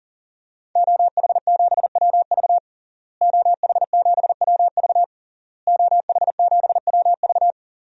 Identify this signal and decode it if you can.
Morse code, heard as a keyed tone, about 40 wpm, 700 Hz. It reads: OH7WV OH7WV OH7WV
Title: DXCC-100 Call Signs - 5 Characters - Repeated 3x - 40wpm